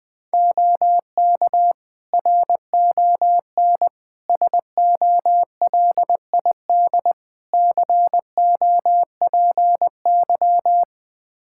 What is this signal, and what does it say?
Morse code, heard as a keyed tone, about 20 wpm, 700 Hz. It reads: OK RON SOLID COPY
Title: Practice QSO - Exchange 3 - More Info - 20wpm